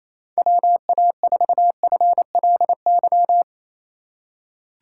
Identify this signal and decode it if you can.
Morse code, heard as a keyed tone, about 28 wpm, 700 Hz. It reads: WA4FLY
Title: USA Call Signs - 2x Character Spacing - 28wpm